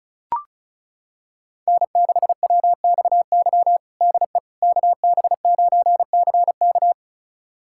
Morse code, heard as a keyed tone, about 36 wpm, 700 Hz. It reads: N6WXY DE KB9CK
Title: USA Callsign DE Callsign - 5 Characters - 35wpm